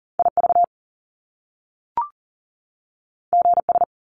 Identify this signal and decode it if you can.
Morse code, heard as a keyed tone, about 41 wpm, 700 Hz. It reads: IV GS
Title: Top Two Letter Combinations in Words 40wpm